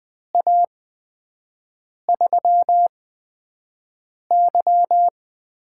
Morse code, heard as a keyed tone, about 20 wpm, spaced as for 5 wpm, 700 Hz. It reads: A3Y
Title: USA Call Signs - 6 Characters - 8x Character Spacing - 20wpm